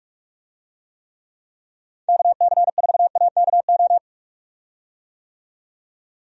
Morse code, heard as a keyed tone, about 43 wpm, 700 Hz. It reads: KC4AKO
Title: USA Call Signs - 6 Characters - 45wpm